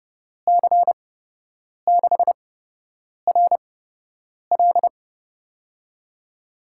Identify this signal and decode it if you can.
Morse code, heard as a keyed tone, about 30 wpm, spaced as for 8 wpm, 700 Hz. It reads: C6RL
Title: USA Call Signs - 5 Characters - 8x Character Spacing - 30wpm